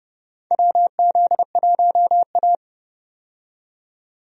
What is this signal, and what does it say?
Morse code, heard as a keyed tone, about 31 wpm, 700 Hz. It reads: WZ1A
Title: USA Call Signs - 4 Characters - 30wpm